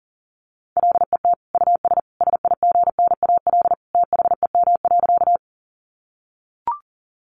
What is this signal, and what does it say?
Morse code, heard as a keyed tone, about 40 wpm, 700 Hz. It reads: LET US SIGNAL THEM.
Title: Sentences from Top 1400 Words 40wpm